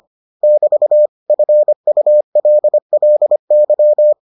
Morse code, heard as a keyed tone, about 25 wpm, 600 Hz. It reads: <BT> FULLY